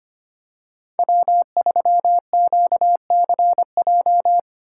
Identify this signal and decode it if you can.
Morse code, heard as a keyed tone, about 25 wpm, 700 Hz. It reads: W3QCJ
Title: USA Call Signs - 2x Character Spacing - 25wpm